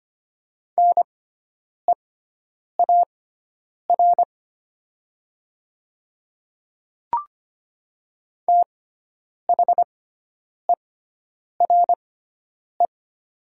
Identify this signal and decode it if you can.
Morse code, heard as a keyed tone, about 25 wpm, spaced as for 9 wpm, 700 Hz. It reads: NEAR THERE
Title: 6x Character Spacing - Top 200 Words - 25wpm